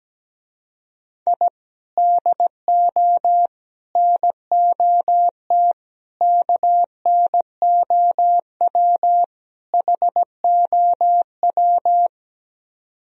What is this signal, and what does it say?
Morse code, heard as a keyed tone, about 17 wpm, 700 Hz. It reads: I DO NOT KNOW HOW
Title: Speed-Racing - Sets of 5 Words 17wpm